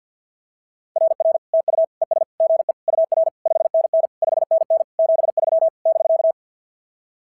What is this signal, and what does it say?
Morse code, heard as a keyed tone, about 44 wpm, 660 Hz. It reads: RR TU ES GE UR 5NN 5NN 73 <BK>
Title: SOTA QSO Practice 50wpm